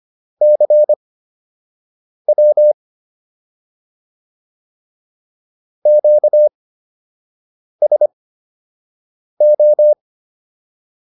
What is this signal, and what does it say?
Morse code, heard as a keyed tone, about 25 wpm, spaced as for 6 wpm, 600 Hz. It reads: CW QSO